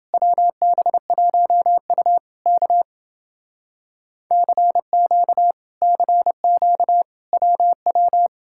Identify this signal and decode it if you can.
Morse code, heard as a keyed tone, about 28 wpm, 700 Hz. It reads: WB1U K CQ CQ WW